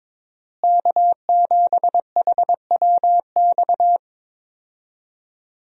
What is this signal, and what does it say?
Morse code, heard as a keyed tone, about 21 wpm, 700 Hz. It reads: K7HWX
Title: USA Call Signs - 5 Characters - Repeated 3x - 22wpm